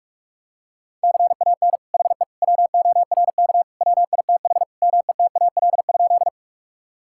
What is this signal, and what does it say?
Morse code, heard as a keyed tone, about 46 wpm, 700 Hz. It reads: CAN HE WORK WITH METAL?
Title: Sentences from Top 1000 Words - Review Part 6 - 45wpm